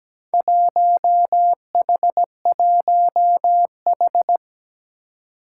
Morse code, heard as a keyed tone, about 17 wpm, 700 Hz. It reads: 1H1H
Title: Four Challenging Letter-Number Combinations 17wpm